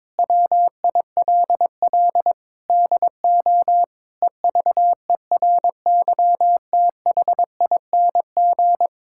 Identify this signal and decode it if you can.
Morse code, heard as a keyed tone, about 22 wpm, 700 Hz. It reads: WILL DO EVERYTHING
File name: practice / 22wpm / Sets of 8 Words - 22wpm